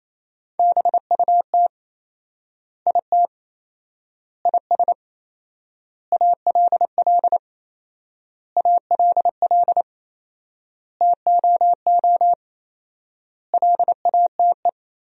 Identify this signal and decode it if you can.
Morse code, heard as a keyed tone, about 28 wpm, 700 Hz. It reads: BUT IT IS ALL ALL TOO LATE
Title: 4x Word-Spacing - Sentences from Top 300 Words - 28wpm